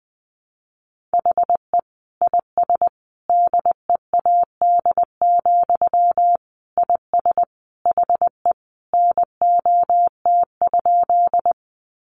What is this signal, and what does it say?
Morse code, heard as a keyed tone, about 20 wpm, 700 Hz. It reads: HE IS DEAD, IS HE NOT?
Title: Sentences from Top 1500 Words 20wpm